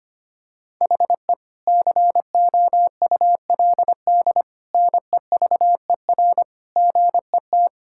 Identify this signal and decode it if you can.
Morse code, heard as a keyed tone, about 25 wpm, 700 Hz. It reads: HE COULD NEVER GET
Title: Sentences from Top 1000 Words - Review Part 5 - 25wpm